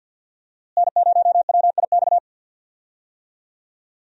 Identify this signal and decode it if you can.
Morse code, heard as a keyed tone, about 52 wpm, 700 Hz. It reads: N0WIX
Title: USA Call Signs - 5 Characters - Repeated 3x - 50wpm